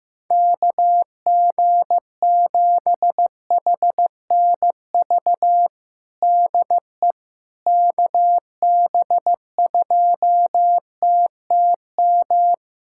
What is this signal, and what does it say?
Morse code, heard as a keyed tone, about 15 wpm, 700 Hz. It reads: KG7HNV DE KB2TTM
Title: USA Callsign DE Callsign - 6 Characters - 15wpm